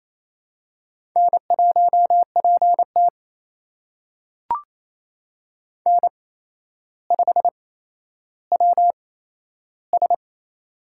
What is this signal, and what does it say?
Morse code, heard as a keyed tone, about 27 wpm, spaced as for 12 wpm, 700 Hz. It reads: N1PT N5WS
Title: USA Call Signs - 4 Characters - 8x Character Spacing - 28wpm